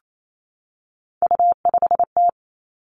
Morse code, heard as a keyed tone, about 28 wpm, 700 Hz. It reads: U5T
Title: Three Challenging Letter-Number Combinations 28wpm